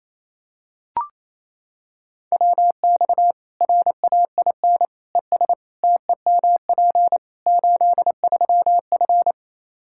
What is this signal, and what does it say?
Morse code, heard as a keyed tone, about 27 wpm, 700 Hz. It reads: WX RAIN ES TEMP 83F
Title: Practice QSO - Exchange 2 - Info - 28wpm